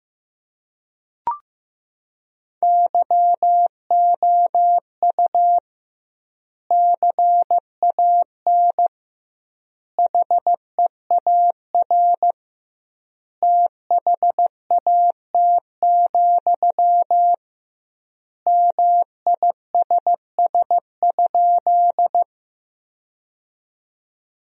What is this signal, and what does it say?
Morse code, heard as a keyed tone, about 15 wpm, 700 Hz. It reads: YOU CAN HEAR THAT, MISS?
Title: 2x Word-Spacing - Sentences from Top 500 Words - 15wpm